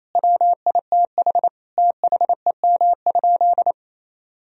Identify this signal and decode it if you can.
Morse code, heard as a keyed tone, about 28 wpm, 700 Hz. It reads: WITH THEM?
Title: Sentences from Top 1000 Words - Review Part 2 - 28wpm